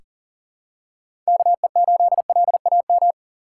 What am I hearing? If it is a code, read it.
Morse code, heard as a keyed tone, about 40 wpm, 700 Hz. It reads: KE8LAM
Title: USA Call Signs - 6 Characters - 3x Character Spacing - 40wpm